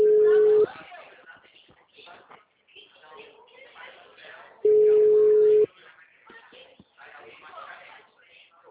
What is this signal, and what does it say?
Ringback tone (Europe, 425 Hz), with speech.